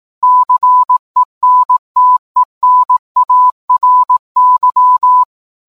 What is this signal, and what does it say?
Morse code, heard as a keyed tone, about 18 wpm, 1000 Hz. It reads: CENTENARY
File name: download-morsecode-titanic